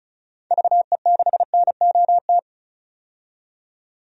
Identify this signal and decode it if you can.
Morse code, heard as a keyed tone, about 35 wpm, 700 Hz. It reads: VE6NOT
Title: DXCC-100 Call Signs - 6 Characters - 35wpm